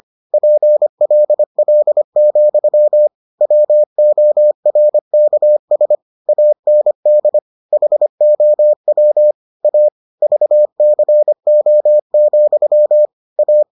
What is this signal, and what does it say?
Morse code, heard as a keyed tone, about 25 wpm, 600 Hz. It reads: PLL, WORKS AND HOW A VCO, A